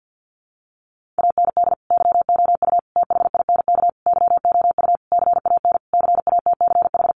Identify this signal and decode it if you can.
Morse code, heard as a keyed tone, about 50 wpm, 700 Hz. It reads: AND YOU THINK YOU CAN CATCH
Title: Sentences from Top 1500 Words 50wpm